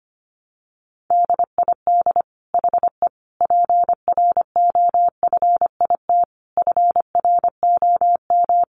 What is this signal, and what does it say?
Morse code, heard as a keyed tone, about 25 wpm, 700 Hz. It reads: DID HE PROFIT FROM
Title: Sentences from Top 800 Words 25wpm